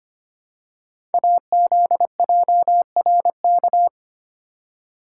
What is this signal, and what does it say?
Morse code, heard as a keyed tone, about 25 wpm, 700 Hz. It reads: AZJRK
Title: Code Groups - 5 Letter:Numbers - 25wpm